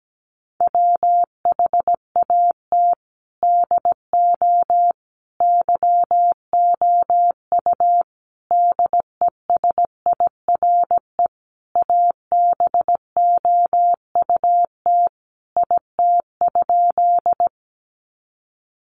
Morse code, heard as a keyed tone, about 17 wpm, 700 Hz. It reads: WHAT DO YOU DESIRE ABOUT IT?
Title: Sentences from Top 1100 Words 17wpm